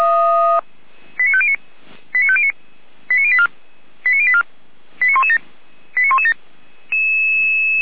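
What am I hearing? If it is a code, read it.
Five-tone selective call (ZVEI), heard as quick runs of five tones, tones of about 70 ms. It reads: 89509 89509 89084 89084 89117 89117
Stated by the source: ZVEI five-tone selective call 89120 received on an SDR.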